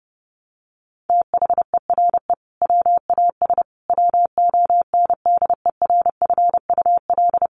Morse code, heard as a keyed tone, about 30 wpm, 700 Hz. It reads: THERE WAS WONDERFUL
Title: Sentences from Top 1200 Words 30wpm